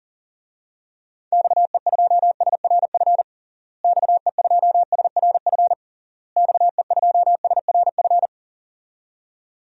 Morse code, heard as a keyed tone, about 40 wpm, 700 Hz. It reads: XE2SRF XE2SRF XE2SRF
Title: DXCC-100 Call Signs - 6 Characters - Repeated 3x - 40wpm